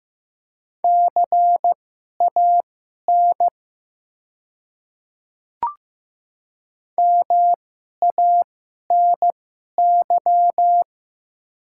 Morse code, heard as a keyed tone, about 15 wpm, spaced as for 11 wpm, 700 Hz. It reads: CAN MANY
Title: 2x Character Spacing - Top 500 Words - 15wpm